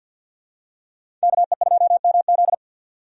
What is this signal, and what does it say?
Morse code, heard as a keyed tone, about 52 wpm, 700 Hz. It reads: KE2MZ
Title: USA Call Signs - 5 Characters - 50wpm